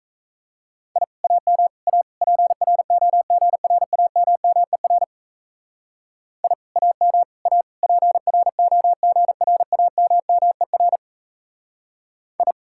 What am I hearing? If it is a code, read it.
Morse code, heard as a keyed tone, about 37 wpm, 700 Hz. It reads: I AM A PROGRAMMER I AM A PROGRAMMER I